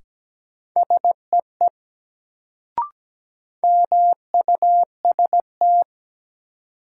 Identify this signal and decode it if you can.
Morse code, heard as a keyed tone, about 17 wpm, 700 Hz. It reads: SEE MUST
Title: Sets of 2 Words - New Word Drill - 17wpm